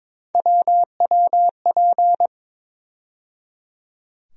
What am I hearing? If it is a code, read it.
Morse code, heard as a keyed tone, about 21 wpm, 700 Hz. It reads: WWP
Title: USA Call Sign Postfixes - 3 Letters 22wpm